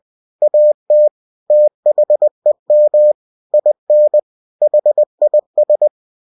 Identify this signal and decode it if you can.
Morse code, heard as a keyed tone, about 20 wpm, 600 Hz. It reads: AT THEM IN HIS
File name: cw-books / winnie-the-pooh / winnie-20wpm-page-0083